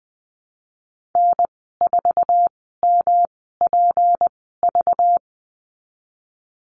Morse code, heard as a keyed tone, about 19 wpm, spaced as for 14 wpm, 700 Hz. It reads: N4MPV
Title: 193 Lesson 31g - Letter v - Call Signs 20wpm